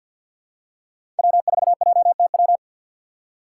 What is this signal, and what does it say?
Morse code, heard as a keyed tone, about 50 wpm, 700 Hz. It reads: W4JTW
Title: USA Call Signs - 5 Characters - 6x Character Spacing - 50wpm